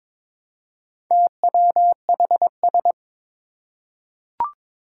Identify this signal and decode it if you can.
Morse code, heard as a keyed tone, about 22 wpm, 700 Hz. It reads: TWHS
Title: Four Challenging Letter-Number Combinations 22wpm